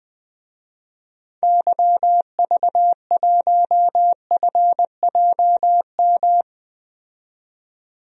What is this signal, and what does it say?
Morse code, heard as a keyed tone, about 20 wpm, 700 Hz. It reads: YV1FJM
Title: DXCC-100 Call Signs - 6 Characters - 20wpm